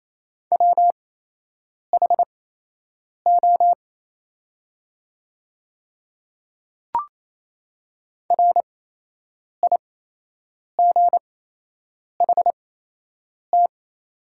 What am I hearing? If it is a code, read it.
Morse code, heard as a keyed tone, about 28 wpm, spaced as for 8 wpm, 700 Hz. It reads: WHO RIGHT
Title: 8x Character Spacing - Top 200 Words - 28wpm